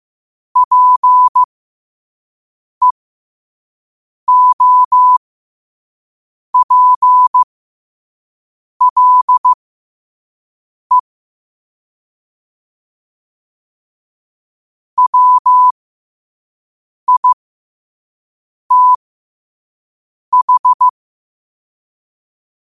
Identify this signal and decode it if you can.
Morse code, heard as a keyed tone, about 15 wpm, spaced as for 5 wpm, 1000 Hz. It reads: PEOPLE WITH
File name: qotdfiles / 5wpm